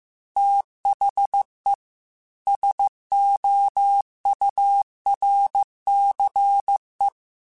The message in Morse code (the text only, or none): THE SOURCE